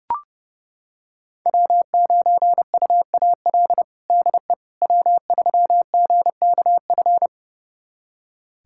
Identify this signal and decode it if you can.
Morse code, heard as a keyed tone, about 32 wpm, 700 Hz. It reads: W9UAL DE W3GKF
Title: USA Callsign DE Callsign - 5 Characters - 30wpm